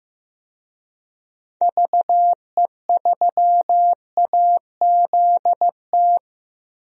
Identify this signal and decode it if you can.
Morse code, heard as a keyed tone, about 15 wpm, 700 Hz. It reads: VE3AZT
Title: DXCC-100 Call Signs - 6 Characters - 15wpm